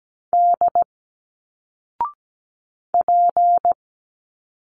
Morse code, heard as a keyed tone, about 17 wpm, 700 Hz. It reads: D P
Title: Single Letters 17wpm